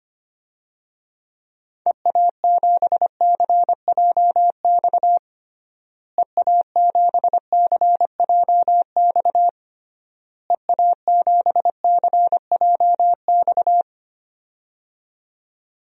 Morse code, heard as a keyed tone, about 25 wpm, 700 Hz. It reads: EA7CJX EA7CJX EA7CJX
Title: DXCC-100 Call Signs - 6 Characters - Repeated 3x - 25wpm